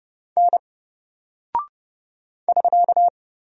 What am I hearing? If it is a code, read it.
Morse code, heard as a keyed tone, about 30 wpm, 700 Hz. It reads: N <SK>
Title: Single Character (Letter:Number:Punctuation:Prosign) 30wpm